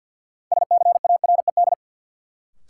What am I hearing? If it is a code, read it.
Morse code, heard as a keyed tone, about 50 wpm, 700 Hz. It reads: SCARED
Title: Sentences from Top 700 Words - New Word Drill - 50wpm